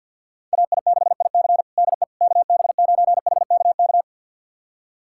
Morse code, heard as a keyed tone, about 50 wpm, 700 Hz. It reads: AI6IC DE KB9HKK
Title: USA Callsign DE Callsign 50wpm